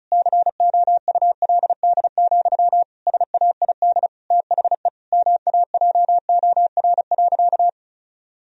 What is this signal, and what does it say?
Morse code, heard as a keyed tone, about 36 wpm, 700 Hz. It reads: COULD, SAID THE MAJOR.
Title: Sentences from Top 1000 Words - Review Part 4 - 35wpm